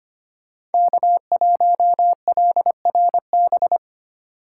Morse code, heard as a keyed tone, about 25 wpm, 700 Hz. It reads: K1LRB
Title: USA Call Signs - 5 Characters - 2x Character Spacing - 25wpm